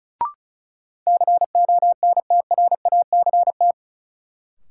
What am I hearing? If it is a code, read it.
Morse code, heard as a keyed tone, about 36 wpm, 700 Hz. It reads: CONTRACT
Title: Sentences from Top 1400 Words - New Word Drill Alt - 35wpm